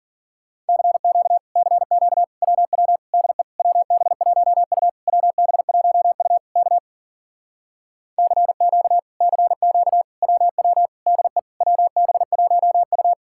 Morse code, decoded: CQ CQ WW DE WB1U WB1U K CQ CQ WW DE WB1U

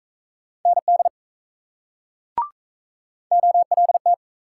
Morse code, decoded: ND OLT

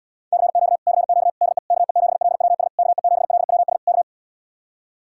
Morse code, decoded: CQ CQ DE KE7LWS KE7LWS K

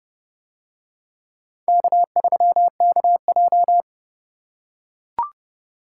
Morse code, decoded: K3KJ

